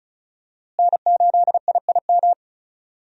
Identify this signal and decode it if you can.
Morse code, heard as a keyed tone, about 34 wpm, 700 Hz. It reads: N8IIM
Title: USA Call Signs - 5 Characters - Repeated 2x - 35wpm